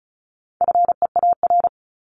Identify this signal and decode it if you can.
Morse code, heard as a keyed tone, about 34 wpm, 700 Hz. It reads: FEAR
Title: Top 1751-2000 Words 35wpm